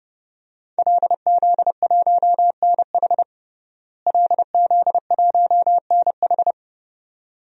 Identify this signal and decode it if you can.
Morse code, heard as a keyed tone, about 30 wpm, 700 Hz. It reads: LZ1NH LZ1NH
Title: DXCC-100 Call Signs - 5 Characters - Repeated 2x - 30wpm